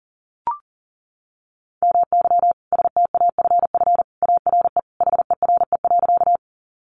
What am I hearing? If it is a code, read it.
Morse code, heard as a keyed tone, about 40 wpm, 700 Hz. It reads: MY STAFF ARE HERE.
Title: Sentences from Top 1200 Words 40wpm